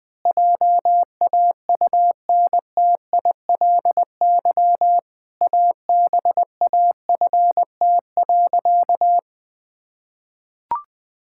Morse code, heard as a keyed tone, about 20 wpm, 700 Hz. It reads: JAUNTILY ABAFT.
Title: Pangram Challenge v2 - 20wpm